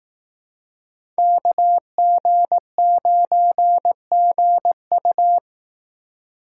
Morse code, heard as a keyed tone, about 18 wpm, 700 Hz. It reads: KG9GU
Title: Speed Racing - USA 2x2 Call Signs 15wpm